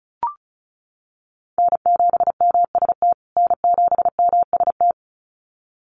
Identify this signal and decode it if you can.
Morse code, heard as a keyed tone, about 34 wpm, 700 Hz. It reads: N7MST N7MST
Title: USA Call Signs Repeated 2x 35wpm